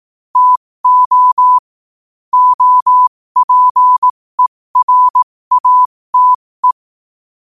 Morse code, heard as a keyed tone, about 18 wpm, spaced as for 15 wpm, 1000 Hz. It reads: TO OPERATE
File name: qotdfiles / 15wpm